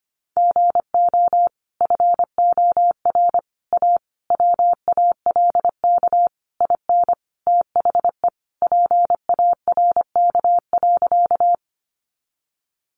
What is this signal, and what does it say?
Morse code, heard as a keyed tone, about 25 wpm, 700 Hz. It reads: GO FOR A WALK IN THE PARK.
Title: Sentences from Top 1300 Words 25wpm